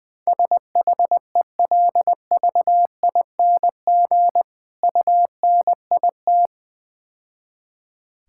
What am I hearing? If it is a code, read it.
Morse code, heard as a keyed tone, about 20 wpm, 700 Hz. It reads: SHELVING UNIT
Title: Easy Words - Speed-Racing with Hints - 20wpm